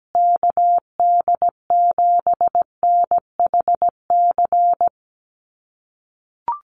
Morse code, heard as a keyed tone, about 17 wpm, 700 Hz. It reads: KD7NHC
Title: USA Call Signs Repeated 3x 17wpm